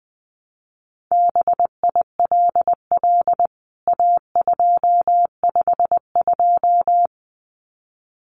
Morse code, heard as a keyed tone, about 20 wpm, 700 Hz. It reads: BILL A252
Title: CWT Exchange 20wpm